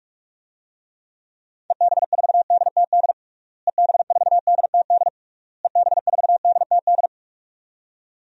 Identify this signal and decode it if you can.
Morse code, heard as a keyed tone, about 46 wpm, 700 Hz. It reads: EB4DTD EB4DTD EB4DTD
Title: DXCC-100 Call Signs - 6 Characters - Repeated 3x - 45wpm